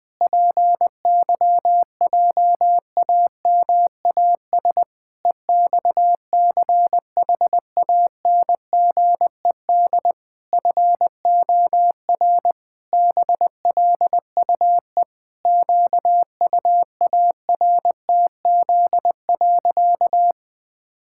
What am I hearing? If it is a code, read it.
Morse code, heard as a keyed tone, about 19 wpm, 700 Hz. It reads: PYJAMAS EXCHANGED FOR BLUE QUARTZ.
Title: Pangram Challenge v2 - 20wpm